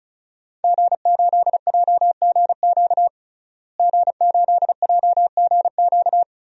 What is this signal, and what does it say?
Morse code, heard as a keyed tone, about 36 wpm, 700 Hz. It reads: G8JGQ G8JGQ
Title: DXCC-100 Call Signs - 5 Characters - Repeated 2x - 35wpm